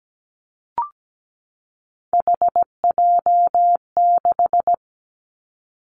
Morse code, heard as a keyed tone, about 17 wpm, 700 Hz. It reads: HJ6
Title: Callsign Prefixes for Top 100 DXCC - 17wpm